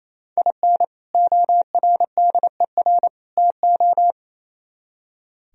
Morse code, heard as a keyed tone, about 32 wpm, 700 Hz. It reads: IN ORDER TO
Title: Speed-Racing - Sets of 3 Words 28wpm